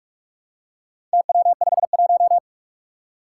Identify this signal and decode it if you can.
Morse code, heard as a keyed tone, about 44 wpm, 700 Hz. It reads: TW51
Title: Four Challenging Letter-Number Combinations 45wpm